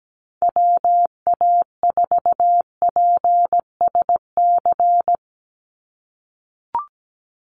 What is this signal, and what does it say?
Morse code, heard as a keyed tone, about 17 wpm, 700 Hz. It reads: WA4PSC
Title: Call Signs - 17wpm